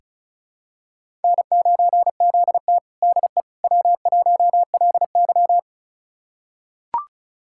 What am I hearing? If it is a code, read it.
Morse code, heard as a keyed tone, about 34 wpm, 690 Hz. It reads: N9ZT DE W1LY
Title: USA Callsign DE Callsign - 4 Characters - 35wpm